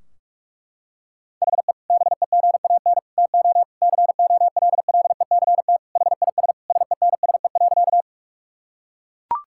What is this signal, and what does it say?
Morse code, heard as a keyed tone, about 42 wpm, 700 Hz. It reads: HE BEGAN TO COLLECT HIS SENSE.
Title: Sentences from Top 1600 Words 45wpm